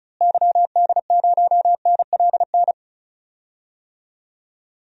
Morse code, heard as a keyed tone, about 36 wpm, 700 Hz. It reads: YD0NLN
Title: DXCC-100 Call Signs - 6 Characters - Repeated 3x - 35wpm